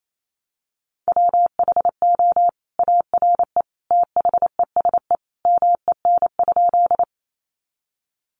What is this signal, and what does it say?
Morse code, heard as a keyed tone, about 28 wpm, 700 Hz. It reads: WHO ARE THESE MEN?
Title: Sentences from Top 200 Words 28wpm